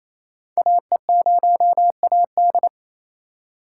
Morse code, heard as a keyed tone, about 28 wpm, 700 Hz. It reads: AE0AD
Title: Code Groups - 5 Letter:Numbers - 28wpm